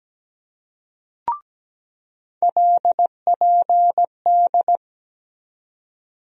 Morse code, heard as a keyed tone, about 17 wpm, 700 Hz. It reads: LPD
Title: USA Call Sign Postfixes - 3 Letters 17wpm